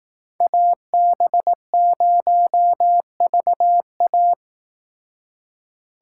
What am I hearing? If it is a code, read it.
Morse code, heard as a keyed tone, about 19 wpm, 700 Hz. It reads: AB0VA